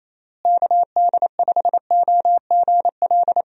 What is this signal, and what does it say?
Morse code, heard as a keyed tone, about 28 wpm, 700 Hz. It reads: KD5OGL
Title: USA Call Signs - 6 Characters - Repeated 3x - 28wpm